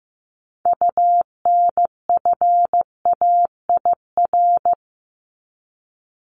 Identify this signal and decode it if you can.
Morse code, heard as a keyed tone, about 15 wpm, 700 Hz. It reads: UNFAIR
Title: Top 1751-2000 Words 15wpm